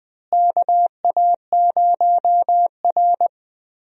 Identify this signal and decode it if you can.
Morse code, heard as a keyed tone, about 20 wpm, 700 Hz. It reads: KA0R
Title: USA Call Signs - 4 Characters - Repeated 3x - 20wpm